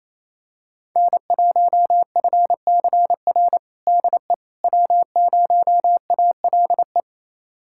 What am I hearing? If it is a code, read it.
Morse code, heard as a keyed tone, about 28 wpm, 700 Hz. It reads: N1FCR DE W0ALE
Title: USA Callsign DE Callsign - 5 Characters - 28wpm